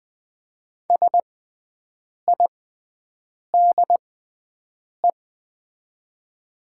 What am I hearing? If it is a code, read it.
Morse code, heard as a keyed tone, about 20 wpm, spaced as for 7 wpm, 700 Hz. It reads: SIDE
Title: 6x Character Spacing - Top 100 Words - 20wpm